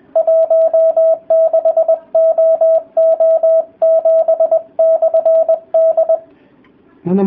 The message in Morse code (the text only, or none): 16OO7/D